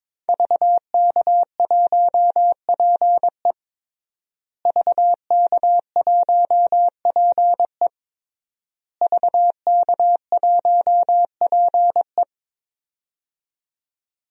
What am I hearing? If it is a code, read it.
Morse code, heard as a keyed tone, about 22 wpm, 700 Hz. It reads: VK1PE VK1PE VK1PE
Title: DXCC-100 Call Signs - 5 Characters - Repeated 3x - 22wpm